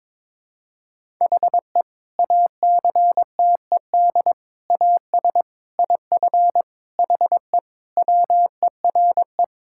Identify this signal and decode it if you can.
Morse code, heard as a keyed tone, about 22 wpm, 700 Hz. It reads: HE ACTED AS IF HE WERE
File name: practice / 22wpm / Sentences with 4-Word Phrases - 22wpm